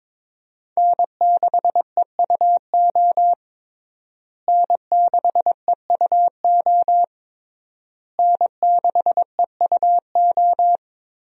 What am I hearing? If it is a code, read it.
Morse code, heard as a keyed tone, about 22 wpm, 700 Hz. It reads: N6EUO N6EUO N6EUO
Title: USA Call Signs - 5 Characters - Repeated 3x - 22wpm